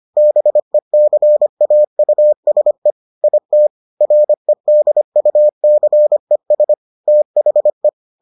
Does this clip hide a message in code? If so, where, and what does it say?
Yes: Morse code from 0.2 to 7.9 s, heard as a keyed tone, about 25 wpm, 600 Hz. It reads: BECAUSE IT REDUCES THE